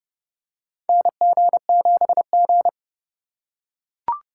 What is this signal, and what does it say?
Morse code, heard as a keyed tone, about 31 wpm, 700 Hz. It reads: NG7G
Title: USA Call Signs - 4 Characters - 2x Character Spacing - 30wpm